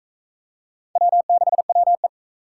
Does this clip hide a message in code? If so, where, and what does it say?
Yes: Morse code from 1.0 to 2.1 s, heard as a keyed tone, about 39 wpm, 700 Hz. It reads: W6WE